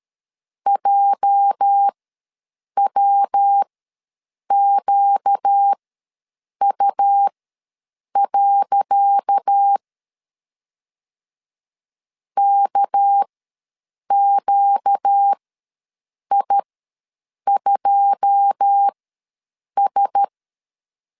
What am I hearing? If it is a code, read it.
Morse code, heard as a keyed tone, about 13 wpm, spaced as for 7 wpm, 800 Hz. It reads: JWQU. KQI2S
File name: LEZIONI / 10 SIGNS / 10C DOT ws 12wpm 3548